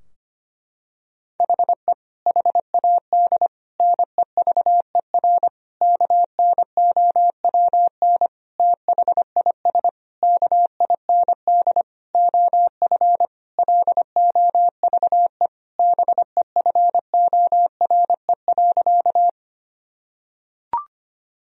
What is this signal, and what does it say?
Morse code, heard as a keyed tone, about 25 wpm, 700 Hz. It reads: HE HAD NEVER KNOWN THIS KIND OF LOVE BEFORE.
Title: Sentences from Top 1000 Words - Review Part 3 - 25wpm